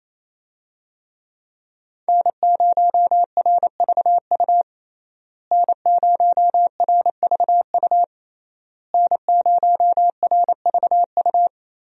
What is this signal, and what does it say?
Morse code, heard as a keyed tone, about 28 wpm, 700 Hz. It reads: N0RVU N0RVU N0RVU
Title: USA Call Signs - 5 Characters - Repeated 3x - 28wpm